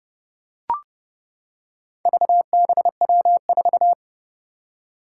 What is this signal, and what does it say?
Morse code, heard as a keyed tone, about 32 wpm, 700 Hz. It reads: VBW4
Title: Four Challenging Letter-Number Combinations 30wpm